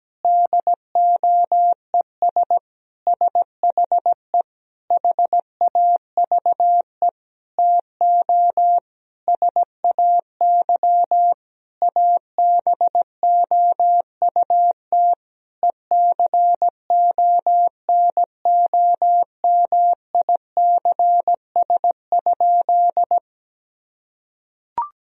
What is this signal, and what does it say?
Morse code, heard as a keyed tone, about 17 wpm, 700 Hz. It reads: DOES SHE HAVE TO SAY ABOUT ECONOMICS?
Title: Sentences from Top 1000 Words - Review Part 3 - 17wpm